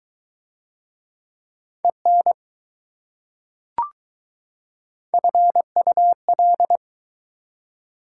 Morse code, heard as a keyed tone, about 20 wpm, 700 Hz. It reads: EN FUL